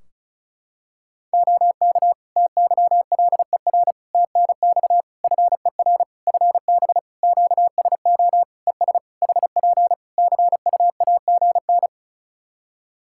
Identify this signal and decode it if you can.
Morse code, heard as a keyed tone, about 36 wpm, 700 Hz. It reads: OK TYLER TNX FER FB QSO ES HP CUAGN
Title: Practice QSO - Exchange 4 - Ending - 35wpm